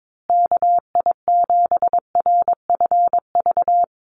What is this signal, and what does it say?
Morse code, heard as a keyed tone, about 22 wpm, 700 Hz. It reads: KI7RFV
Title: USA Call Signs Repeated 2x 22wpm